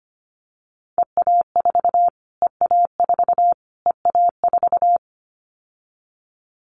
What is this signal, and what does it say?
Morse code, heard as a keyed tone, about 25 wpm, 700 Hz. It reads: EA4 EA4 EA4
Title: Callsign Prefixes for Top 25 DXCC - Repeated 3x - 25wpm